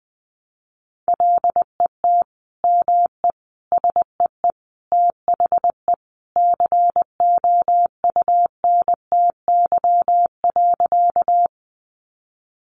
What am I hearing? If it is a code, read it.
Morse code, heard as a keyed tone, about 19 wpm, 700 Hz. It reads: LET ME SEE THE COUNTY.
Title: Sentences from Top 1200 Words 20wpm